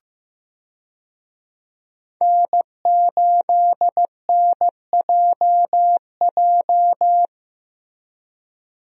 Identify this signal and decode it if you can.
Morse code, heard as a keyed tone, about 15 wpm, 700 Hz. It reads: N8NJJ
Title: USA Call Signs - 5 Characters - Repeated 2x - 15wpm